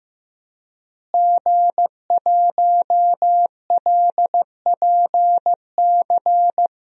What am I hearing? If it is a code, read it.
Morse code, heard as a keyed tone, about 15 wpm, 700 Hz. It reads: G1LPC
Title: DXCC-100 Call Signs - 5 Characters - Repeated 2x - 15wpm